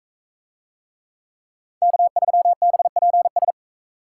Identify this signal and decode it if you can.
Morse code, heard as a keyed tone, about 42 wpm, 700 Hz. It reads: K3BPS